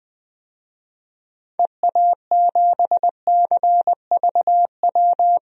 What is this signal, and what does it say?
Morse code, heard as a keyed tone, about 20 wpm, 700 Hz. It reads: EA7CVW
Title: DXCC-100 Call Signs - 6 Characters - Repeated 3x - 20wpm